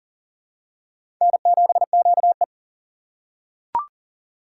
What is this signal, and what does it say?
Morse code, heard as a keyed tone, about 39 wpm, 700 Hz. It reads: N7QE
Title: USA Call Signs - 4 Characters - 3x Character Spacing - 40wpm